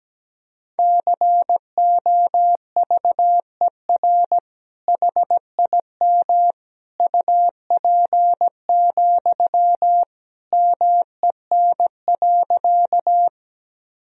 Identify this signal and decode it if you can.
Morse code, heard as a keyed tone, about 17 wpm, 700 Hz. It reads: COVER HIM UP, MEN.
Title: Sentences from Top 1000 Words - Review Part 1 - 17wpm